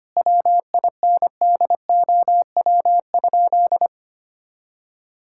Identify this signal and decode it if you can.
Morse code, heard as a keyed tone, about 25 wpm, 700 Hz. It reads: WINDOW?
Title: Sentences from Top 1600 Words 25wpm